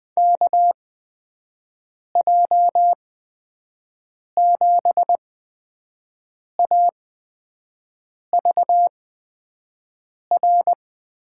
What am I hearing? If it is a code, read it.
Morse code, heard as a keyed tone, about 20 wpm, spaced as for 5 wpm, 700 Hz. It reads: KJ7AVR
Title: USA Call Signs - 6 Characters - 8x Character Spacing - 20wpm